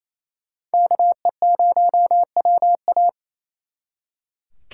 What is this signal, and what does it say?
Morse code, heard as a keyed tone, about 36 wpm, 700 Hz. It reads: KE0WA